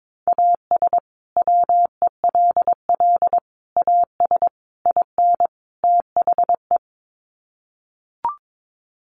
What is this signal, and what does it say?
Morse code, heard as a keyed tone, about 22 wpm, 700 Hz. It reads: AS WELL AS IN THE
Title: Sets of 5 Words Encore 22wpm